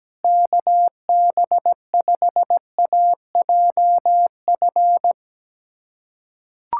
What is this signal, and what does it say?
Morse code, heard as a keyed tone, about 17 wpm, 700 Hz. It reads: KB5AJF
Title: USA Callsign DE Callsign 17wpm